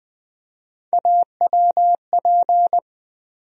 Morse code, heard as a keyed tone, about 20 wpm, 700 Hz. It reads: AWP
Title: USA Call Sign Postfixes - 3 Letters 20wpm